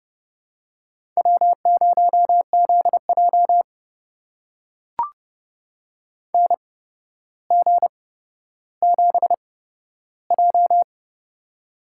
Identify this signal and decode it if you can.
Morse code, heard as a keyed tone, about 30 wpm, spaced as for 13 wpm, 700 Hz. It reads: W0ZJ NG7J
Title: USA Call Signs - 4 Characters - 8x Character Spacing - 30wpm